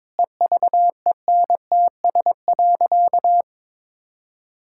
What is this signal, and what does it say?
Morse code, heard as a keyed tone, about 22 wpm, 700 Hz. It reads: EVENTS.
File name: practice / 22wpm / Sentences with 8-Word Phrases - 22wpm